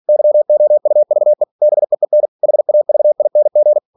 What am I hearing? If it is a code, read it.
Morse code, heard as a keyed tone, about 46 wpm, 600 Hz. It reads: YOUVE BEEN HAVING
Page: Never 47wpm